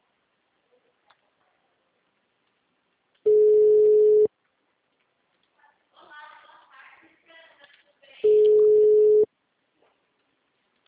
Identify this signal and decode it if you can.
Ringback tone (Europe, 425 Hz), with speech.